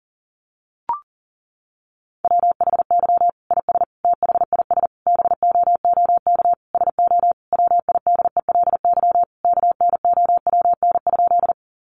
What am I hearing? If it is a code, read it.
Morse code, heard as a keyed tone, about 40 wpm, 700 Hz. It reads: WHY IS THIS BOOK SO WIDELY KNOWN?
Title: Sentences from Top 1200 Words 40wpm